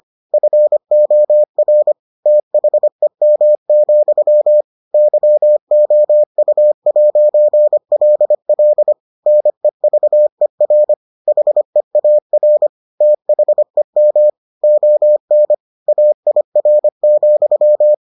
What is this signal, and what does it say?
Morse code, heard as a keyed tone, about 25 wpm, 600 Hz. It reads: FOR THEM, YOU'LL NEVER HEAR THEM ON AIR,